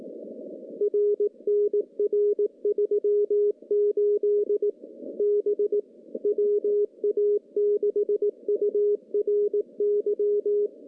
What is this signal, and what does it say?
Morse code, heard as a keyed tone, about 18 wpm, 410 Hz. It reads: R NR38 B WA6URY
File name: CW-SS-k5tr